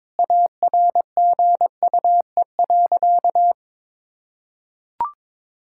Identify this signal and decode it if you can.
Morse code, heard as a keyed tone, about 22 wpm, 700 Hz. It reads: ARGUE.
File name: Sentences with 7-Word Phrases - 22wpm